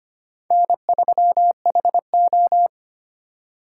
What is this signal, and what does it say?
Morse code, heard as a keyed tone, about 25 wpm, 700 Hz. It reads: N3HO
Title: USA Call Signs - 3x Character Spacing - 25wpm